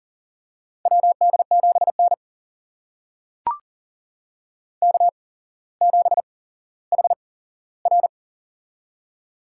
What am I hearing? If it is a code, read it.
Morse code, heard as a keyed tone, about 40 wpm, spaced as for 17 wpm, 700 Hz. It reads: WD7N K7HR
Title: USA Call Signs - 4 Characters - 8x Character Spacing - 40wpm